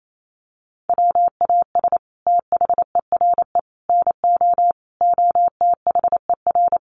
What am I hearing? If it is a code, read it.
Morse code, heard as a keyed tone, about 28 wpm, 700 Hz. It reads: WAS THERE NO OTHER
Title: Sentences from Top 1400 Words 28wpm